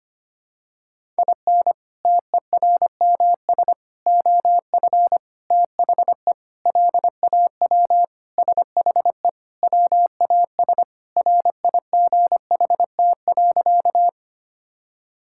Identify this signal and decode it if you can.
Morse code, heard as a keyed tone, about 25 wpm, 700 Hz. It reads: IN TERMS OF THE LAW SHE WAS RIGHT.
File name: practice / 25wpm / Sentences with 4-Word Phrases - 25wpm